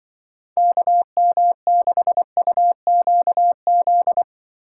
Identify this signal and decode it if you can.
Morse code, heard as a keyed tone, about 24 wpm, 700 Hz. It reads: KM6UQZ
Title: Speed Racing - USA 2x3 Call Signs 20wpm